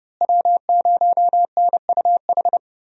Morse code, heard as a keyed tone, about 30 wpm, 700 Hz. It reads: W0NUH
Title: USA Call Signs - 5 Characters - Repeated 2x - 30wpm